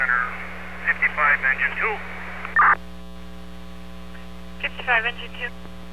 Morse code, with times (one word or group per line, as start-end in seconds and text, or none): none